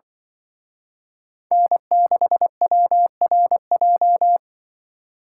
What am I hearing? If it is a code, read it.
Morse code, heard as a keyed tone, about 24 wpm, 700 Hz. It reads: N6WRJ
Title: Speed Racing - USA 1x3 Call Signs 20wpm